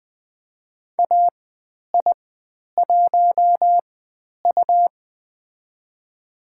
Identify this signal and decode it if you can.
Morse code, heard as a keyed tone, about 20 wpm, spaced as for 10 wpm, 700 Hz. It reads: AI1U
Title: 102 Lesson 17g - Letter u - Call Signs 20wpm